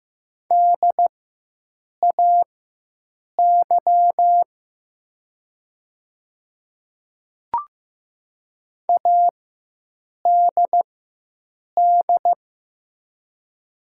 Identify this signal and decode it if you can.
Morse code, heard as a keyed tone, about 15 wpm, spaced as for 7 wpm, 700 Hz. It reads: DAY ADD
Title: 4x Character Spacing - Top 300 Words - 15wpm